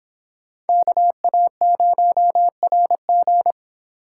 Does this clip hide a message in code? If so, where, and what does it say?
Yes: Morse code from 0.7 to 3.5 s, heard as a keyed tone, about 26 wpm, 700 Hz. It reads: KA0RG